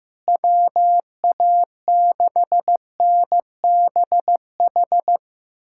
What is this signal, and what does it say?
Morse code, heard as a keyed tone, about 15 wpm, 700 Hz. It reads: WA6NBH
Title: USA Callsign DE Callsign - 6 Characters - 15wpm